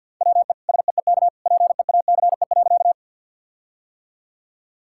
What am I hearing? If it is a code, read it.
Morse code, heard as a keyed tone, about 50 wpm, 700 Hz. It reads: WE SEEK PEACE.
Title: Sentences from Top 1600 Words 50wpm